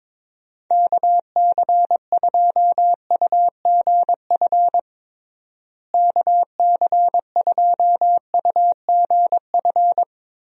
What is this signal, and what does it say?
Morse code, heard as a keyed tone, about 22 wpm, 700 Hz. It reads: KC2UGF KC2UGF
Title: USA Call Signs - 6 Characters - Repeated 2x - 22wpm